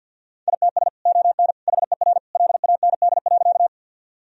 Morse code, heard as a keyed tone, about 50 wpm, 700 Hz. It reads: ITS ON HER LAND.
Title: Sentences from Top 1000 Words - Review Part 1 - 50wpm